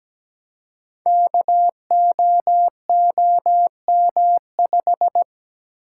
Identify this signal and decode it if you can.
Morse code, heard as a keyed tone, about 17 wpm, 700 Hz. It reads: KOOM5
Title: Code Groups - 5 Letter:Numbers - 17wpm